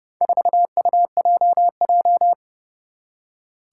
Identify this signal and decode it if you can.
Morse code, heard as a keyed tone, about 30 wpm, 700 Hz. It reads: 4UJJ
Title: Four Challenging Letter-Number Combinations 30wpm